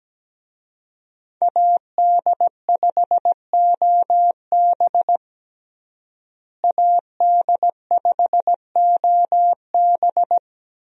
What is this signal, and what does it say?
Morse code, heard as a keyed tone, about 17 wpm, 700 Hz. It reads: AD5OB AD5OB
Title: USA Call Signs - 5 Characters - Repeated 2x - 17wpm